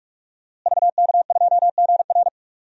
Morse code, heard as a keyed tone, about 47 wpm, 700 Hz. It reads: VK2GR
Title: DXCC-100 Call Signs - 5 Characters - Repeated 3x - 45wpm